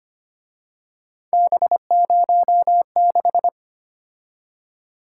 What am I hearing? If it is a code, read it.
Morse code, heard as a keyed tone, about 29 wpm, 700 Hz. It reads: B06
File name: Speed-Racing - Three Letter-Number Combinations - 25wpm